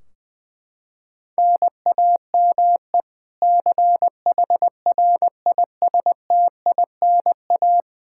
Morse code, heard as a keyed tone, about 20 wpm, 700 Hz. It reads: NAME CHRISTINA
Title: K1USN SST QSO Practice 20wpm